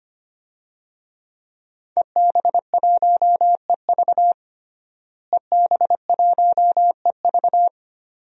Morse code, heard as a keyed tone, about 25 wpm, 700 Hz. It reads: EB1EV EB1EV
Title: DXCC-100 Call Signs - 5 Characters - Repeated 2x - 25wpm